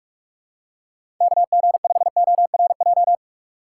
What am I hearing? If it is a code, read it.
Morse code, heard as a keyed tone, about 43 wpm, 700 Hz. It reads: KG5ORJ
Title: USA Call Signs - 6 Characters - 2x Character Spacing - 45wpm